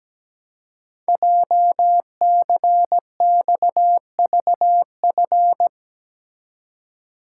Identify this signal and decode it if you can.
Morse code, heard as a keyed tone, about 17 wpm, 700 Hz. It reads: JCXVF
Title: Code Groups - 5 Letters - 17wpm